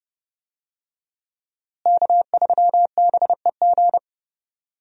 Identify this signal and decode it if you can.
Morse code, heard as a keyed tone, about 30 wpm, 700 Hz. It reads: K3BEG
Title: USA Call Signs - 5 Characters - Repeated 2x - 30wpm